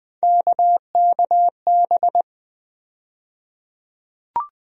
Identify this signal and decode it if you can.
Morse code, heard as a keyed tone, about 20 wpm, 700 Hz. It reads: KKB
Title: USA Call Sign Postfixes - 3 Letters 20wpm